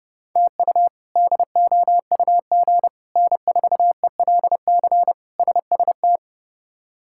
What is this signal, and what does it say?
Morse code, heard as a keyed tone, about 32 wpm, 700 Hz. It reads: TU DOUG N4ELC SST
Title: K1USN SST QSO Practice - 30wpm